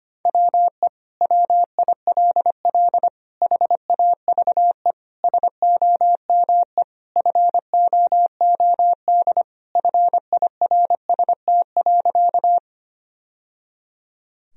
Morse code, heard as a keyed tone, about 25 wpm, 700 Hz. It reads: WE WILL HAVE SOME FOOD FIRST.
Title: Sentences from Top 1000 Words - Review Part 1 - 25wpm